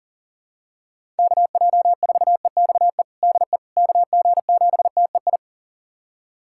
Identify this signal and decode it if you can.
Morse code, heard as a keyed tone, about 40 wpm, 700 Hz. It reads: KJ4EXE DE KG7TEI
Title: USA Callsign DE Callsign - 6 Characters - 40wpm